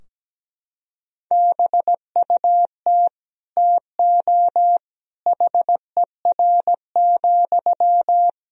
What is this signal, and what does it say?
Morse code, heard as a keyed tone, about 17 wpm, 700 Hz. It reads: BUT TO HER,
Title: Sentences from Top 1000 Words - Review Part 2 - 17wpm